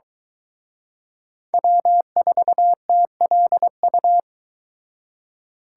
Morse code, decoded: W4TLU

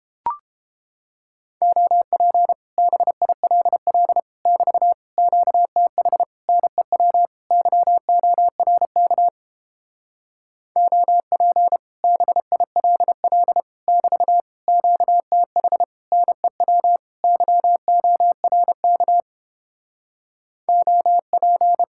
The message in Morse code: OP BILL <BT> QTH NEW YORK OP BILL <BT> QTH NEW YORK OP

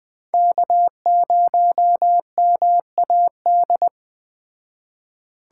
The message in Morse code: K0MAD

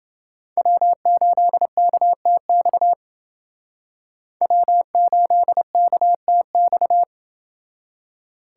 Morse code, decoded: W8KTX W8KTX

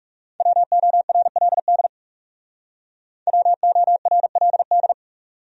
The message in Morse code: WORLD WORLD

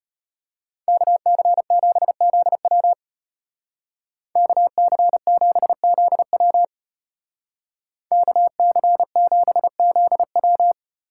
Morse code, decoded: KC7ZW KC7ZW KC7ZW

